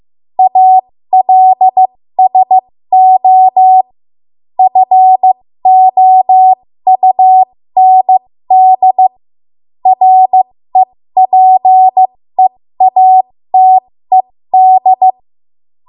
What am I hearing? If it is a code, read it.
Morse code, heard as a keyed tone, about 15 wpm, spaced as for 13 wpm, 750 Hz. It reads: ALSO FOUND REPEATED
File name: Morse / 13WPM